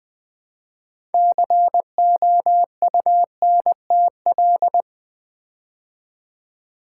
Morse code, heard as a keyed tone, about 20 wpm, 700 Hz. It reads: COUNTL
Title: Build A Word - 3 Letters At A Time - 9-Letter Words - 20wpm